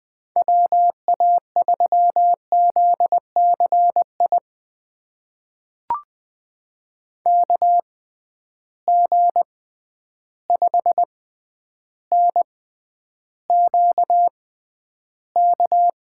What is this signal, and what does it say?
Morse code, heard as a keyed tone, about 20 wpm, spaced as for 10 wpm, 700 Hz. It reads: WA3ZCI KG5NQK
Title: USA Call Signs - 6 Characters - 6x Character Spacing - 20wpm